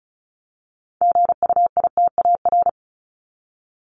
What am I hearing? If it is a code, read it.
Morse code, heard as a keyed tone, about 34 wpm, 700 Hz. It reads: GUITAR
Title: Top 1501-1750 Words 35wpm